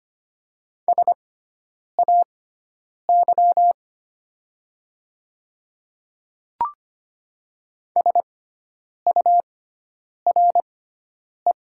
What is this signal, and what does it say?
Morse code, heard as a keyed tone, about 25 wpm, spaced as for 9 wpm, 700 Hz. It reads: SAY SURE
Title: 6x Character Spacing - Top 300 Words - 25wpm